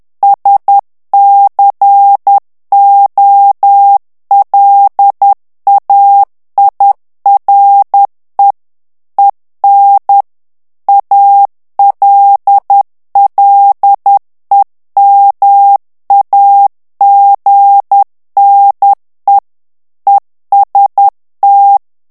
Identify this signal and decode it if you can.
Morse code, heard as a keyed tone, about 10 wpm, 800 Hz. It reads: SCOLAIRE EN ALLEMAGNE EST
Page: Français - texte en clair 1 - vitesse : 10 mots minute